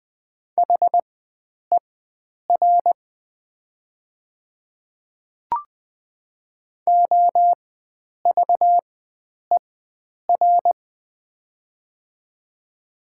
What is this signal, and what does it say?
Morse code, heard as a keyed tone, about 20 wpm, spaced as for 9 wpm, 700 Hz. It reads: HER OVER
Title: 4x Character Spacing - Top 100 Words - 20wpm